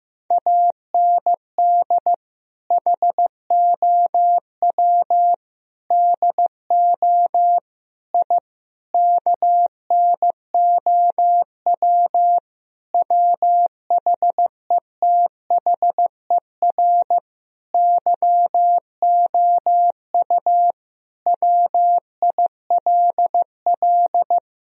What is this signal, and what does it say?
Morse code, heard as a keyed tone, about 15 wpm, 700 Hz. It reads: AND HOW DO I KNOW WHETHER YOU WILL
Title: Sentences from Top 1000 Words - Review Part 3 - 15wpm